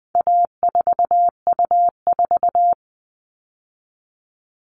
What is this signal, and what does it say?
Morse code, heard as a keyed tone, about 20 wpm, 700 Hz. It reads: A4U4
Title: All Cut Numbers - 4-Digit 20wpm